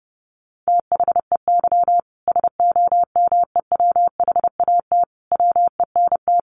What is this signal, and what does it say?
Morse code, heard as a keyed tone, about 30 wpm, 700 Hz. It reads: THEY SOMEWHAT WENT
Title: Sentences from Top 1300 Words 30wpm